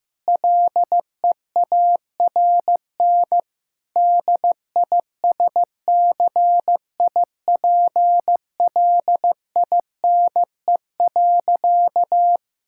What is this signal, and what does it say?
Morse code, heard as a keyed tone, about 15 wpm, 700 Hz. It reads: LEARN DISCIPLINE.
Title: Sentences from Top 1000 Words - Review Part 6 - 15wpm